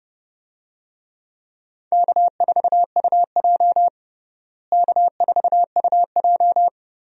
Morse code, heard as a keyed tone, about 31 wpm, 700 Hz. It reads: K4UJ K4UJ
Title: USA Call Signs - 4 Characters - Repeated 3x - 30wpm